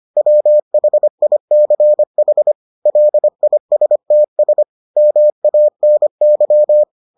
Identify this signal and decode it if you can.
Morse code, heard as a keyed tone, about 25 wpm, 600 Hz. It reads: WHICH LISTS MANY